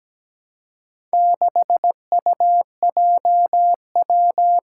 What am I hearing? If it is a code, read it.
Morse code, heard as a keyed tone, about 17 wpm, 700 Hz. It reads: 6UJW
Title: Four Challenging Letter-Number Combinations 17wpm